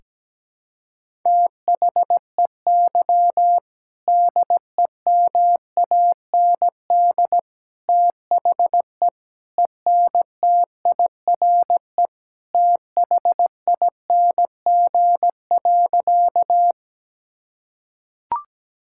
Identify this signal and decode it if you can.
Morse code, heard as a keyed tone, about 17 wpm, 700 Hz. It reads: THEY DEMAND THE ENTIRE THING.
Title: Sentences from Top 1000 Words - Review Part 5 - 17wpm